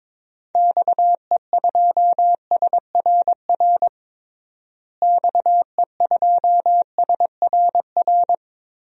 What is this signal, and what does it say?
Morse code, heard as a keyed tone, about 22 wpm, 700 Hz. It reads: XE2SRR XE2SRR
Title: DXCC-100 Call Signs - 6 Characters - Repeated 3x - 22wpm